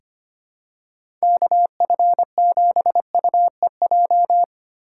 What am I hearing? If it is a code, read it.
Morse code, heard as a keyed tone, about 25 wpm, 700 Hz. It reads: KF7UEJ
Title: USA Call Signs - 6 Characters - 2x Character Spacing - 25wpm